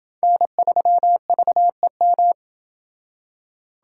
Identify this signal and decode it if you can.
Morse code, heard as a keyed tone, about 25 wpm, 700 Hz. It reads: N3VEM
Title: Speed Racing - USA 1x3 Call Signs 20wpm